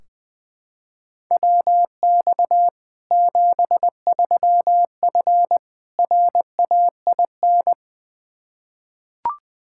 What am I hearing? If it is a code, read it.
Morse code, heard as a keyed tone, about 20 wpm, 700 Hz. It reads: WX 73F RAIN
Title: Practice QSO - Exchange 2 - Info - 20wpm